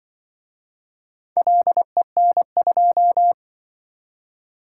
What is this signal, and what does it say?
Morse code, heard as a keyed tone, about 24 wpm, 700 Hz. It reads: LEN2
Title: Speed-Racing - Four Letter-Number Combinations - 20wpm